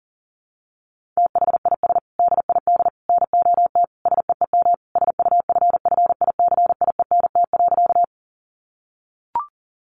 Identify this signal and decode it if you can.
Morse code, heard as a keyed tone, about 40 wpm, 700 Hz. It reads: THIS DID NOT SEEM SUFFICIENT.
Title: Sentences from Top 1400 Words 40wpm